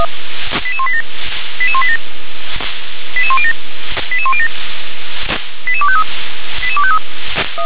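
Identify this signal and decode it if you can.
Five-tone selective call (ZVEI), heard as quick runs of five tones, tones of about 70 ms. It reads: 88187 88187 88107 88107 88253 88253
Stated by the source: ZVEI five-tone call 88161 received on SDR.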